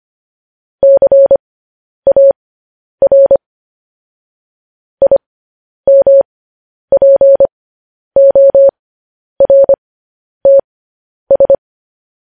Morse code, heard as a keyed tone, about 25 wpm, spaced as for 10 wpm, 550 Hz. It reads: CAR IMPORTS
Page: news-headlines-in-morse-code-at-10-wpm